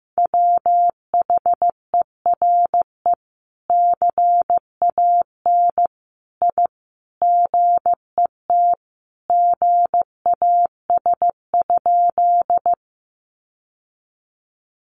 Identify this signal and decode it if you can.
Morse code, heard as a keyed tone, about 15 wpm, 700 Hz. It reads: WHERE CAN I GET GAS?
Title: Sentences from Top 1100 Words 15wpm